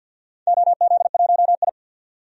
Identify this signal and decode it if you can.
Morse code, heard as a keyed tone, about 52 wpm, 700 Hz. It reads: OZ1I
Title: DXCC-100 Call Signs - 4 Characters - 50wpm